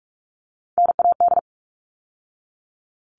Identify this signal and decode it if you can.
Morse code, heard as a keyed tone, about 44 wpm, 700 Hz. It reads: NAD
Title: All Cut Numbers - 3-Digit 45wpm